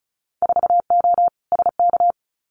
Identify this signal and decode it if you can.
Morse code, heard as a keyed tone, about 35 wpm, 700 Hz. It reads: 4O SK
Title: Winter Field Day Exchanges Encore 35wpm